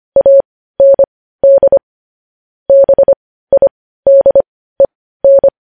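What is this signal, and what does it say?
Morse code, heard as a keyed tone, about 25 wpm, spaced as for 15 wpm, 550 Hz. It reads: AND BIDEN
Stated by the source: News headlines sent as Morse practice at 15.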